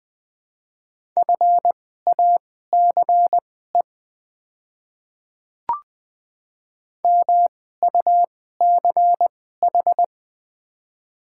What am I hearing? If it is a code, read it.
Morse code, heard as a keyed tone, about 20 wpm, spaced as for 14 wpm, 700 Hz. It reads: FACE MUCH
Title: 2x Character Spacing - Top 500 Words - 20wpm